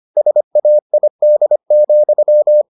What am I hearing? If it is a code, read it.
Morse code, heard as a keyed tone, about 25 wpm, 600 Hz. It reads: SAID,